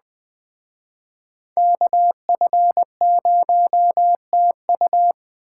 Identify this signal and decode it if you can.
Morse code, heard as a keyed tone, about 19 wpm, 700 Hz. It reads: KF0TU